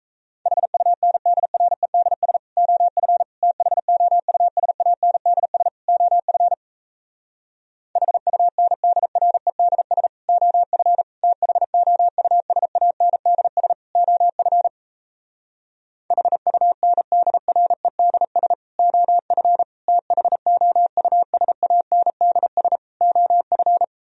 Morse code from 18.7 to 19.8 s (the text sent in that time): OF